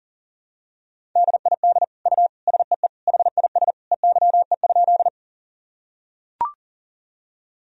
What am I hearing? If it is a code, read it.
Morse code, heard as a keyed tone, about 40 wpm, 700 Hz. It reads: DID U SEE HIS EYE?
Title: Sentences from Top 300 Words and Abbreviations 40wpm